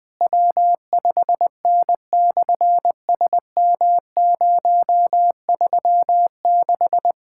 Morse code, heard as a keyed tone, about 20 wpm, 700 Hz. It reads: W5N/SM036
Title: SOTA QSO Practice 20wpm